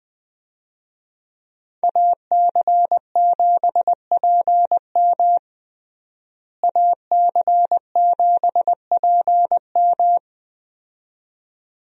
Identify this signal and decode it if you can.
Morse code, heard as a keyed tone, about 20 wpm, 700 Hz. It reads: AC7PM AC7PM
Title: USA Call Signs - 5 Characters - Repeated 2x - 20wpm